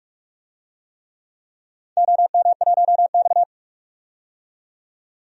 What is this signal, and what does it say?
Morse code, heard as a keyed tone, about 47 wpm, 700 Hz. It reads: OM1X
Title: DXCC-100 Call Signs - 4 Characters - 45wpm